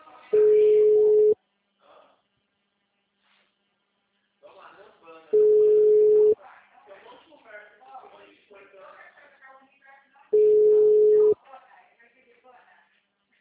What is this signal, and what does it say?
Ringback tone (Europe, 425 Hz), with speech.